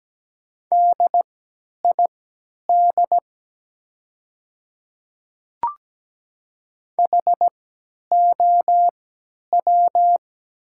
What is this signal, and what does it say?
Morse code, heard as a keyed tone, about 17 wpm, spaced as for 10 wpm, 700 Hz. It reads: DID HOW
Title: 3x Character Spacing - Top 100 Words - 17wpm